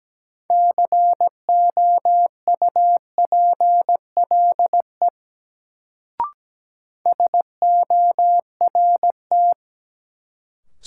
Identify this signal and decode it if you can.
Morse code, heard as a keyed tone, about 17 wpm, 700 Hz. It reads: COUPLE SORT
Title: Sets of 2 Words - New Word Drill - 17wpm